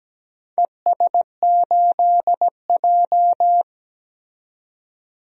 Morse code, heard as a keyed tone, about 17 wpm, 700 Hz. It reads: ES8J
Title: DXCC-100 Call Signs - 4 Characters - 17wpm